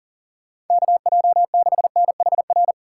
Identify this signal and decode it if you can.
Morse code, heard as a keyed tone, about 49 wpm, 700 Hz. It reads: KJ6NHR
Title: Speed Racing - USA 2x3 Call Signs 40wpm